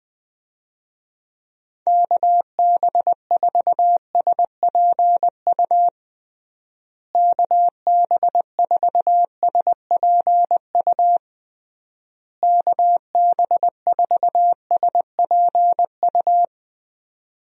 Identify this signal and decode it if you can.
Morse code, heard as a keyed tone, about 21 wpm, 700 Hz. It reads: KB4SPU KB4SPU KB4SPU
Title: USA Call Signs - 6 Characters - Repeated 3x - 20wpm